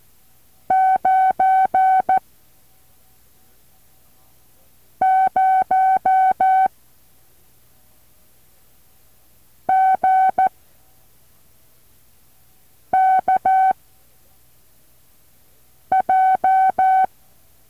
Morse code, heard as a keyed tone, about 14 wpm, 760 Hz. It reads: 9 0 G K J
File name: intl morse code2